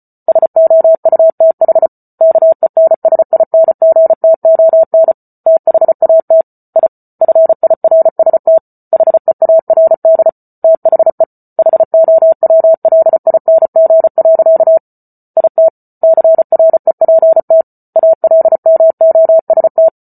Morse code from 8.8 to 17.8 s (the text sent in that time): HEARD THE HOWLING. IT CREPT